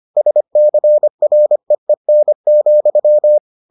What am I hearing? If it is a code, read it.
Morse code, heard as a keyed tone, about 25 wpm, 600 Hz. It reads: SCREEN,